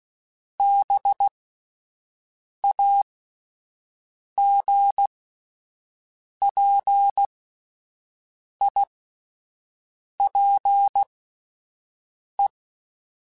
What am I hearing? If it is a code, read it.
Morse code, heard as a keyed tone, about 16 wpm, spaced as for 5 wpm, 800 Hz. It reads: BAGPIPE